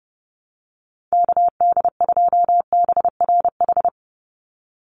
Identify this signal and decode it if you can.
Morse code, heard as a keyed tone, about 29 wpm, 700 Hz. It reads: KD2BRH
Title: USA Call Signs 30wpm